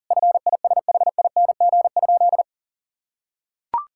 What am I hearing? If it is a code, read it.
Morse code, heard as a keyed tone, about 37 wpm, 700 Hz. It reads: FISHING?
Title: Sentences from Top 1000 Words - Review Part 5 - 40wpm